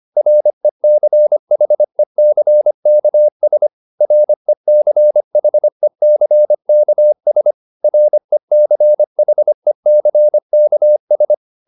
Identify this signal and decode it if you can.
Morse code, heard as a keyed tone, about 25 wpm, 600 Hz. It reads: RECHECKS RECHECKS RECHECKS